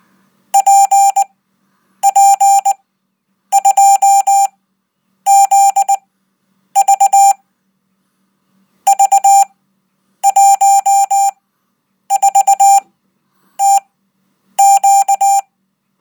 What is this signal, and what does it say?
Morse code, heard as a keyed tone, about 20 wpm, spaced as for 9 wpm, 790 Hz. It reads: PP2ZV V14TQ